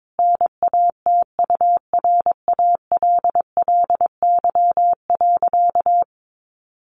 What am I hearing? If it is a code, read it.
Morse code, heard as a keyed tone, about 22 wpm, 700 Hz. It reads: NATURALLY.
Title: Sentences from Top 1300 Words 22wpm